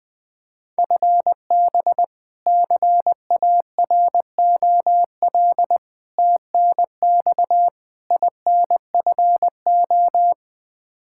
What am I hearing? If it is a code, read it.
Morse code, heard as a keyed tone, about 20 wpm, 700 Hz. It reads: FB CAROL TNX INFO
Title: K1USN SST QSO Practice 20wpm